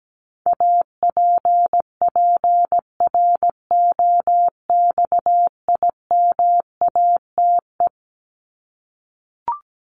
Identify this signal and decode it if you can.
Morse code, heard as a keyed tone, about 17 wpm, 700 Hz. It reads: APPROXIMATE
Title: LY Suffix - Top 100 Words with Base Words 17wpm